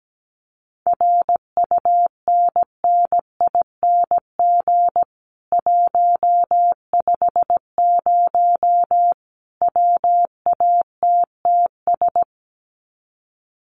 Morse code, heard as a keyed tone, about 17 wpm, 700 Hz. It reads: RUNNING 150 WATTS
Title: Common QSO Phrases 17wpm